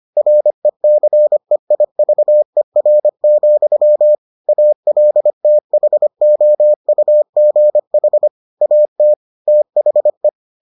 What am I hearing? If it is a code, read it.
Morse code, heard as a keyed tone, about 25 wpm, 600 Hz. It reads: RECEIVER, ALTHOUGH AT THE